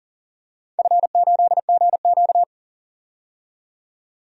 Morse code, heard as a keyed tone, about 38 wpm, 700 Hz. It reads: F8GQ
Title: DXCC-100 Call Signs - 4 Characters - 40wpm